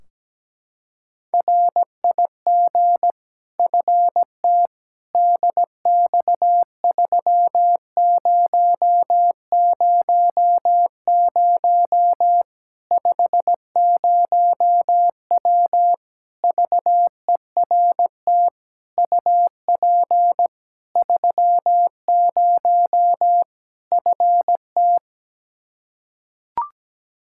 Morse code, heard as a keyed tone, about 17 wpm, 700 Hz. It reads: RIG FT DX3000 50W VERT UP 30 FT
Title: Practice QSO - Exchange 2 - Info - 17wpm